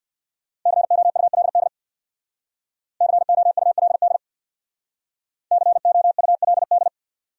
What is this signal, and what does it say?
Morse code, heard as a keyed tone, about 64 wpm, 700 Hz. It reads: COULD COULD COULD